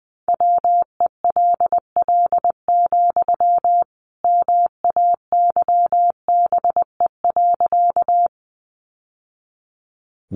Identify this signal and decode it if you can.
Morse code, heard as a keyed tone, about 20 wpm, 700 Hz. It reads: WELL, MAYBE.
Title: Sentences from Top 1000 Words 20wpm